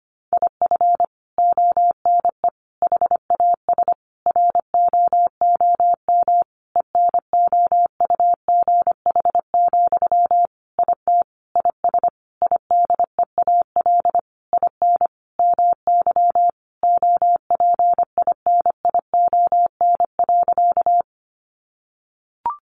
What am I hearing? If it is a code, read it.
Morse code, heard as a keyed tone, about 25 wpm, 700 Hz. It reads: IF ONE HAS ROOM ENOUGH, IT IS IDEAL IN MY OPINION.
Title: Sentences from Top 1400 Words 25wpm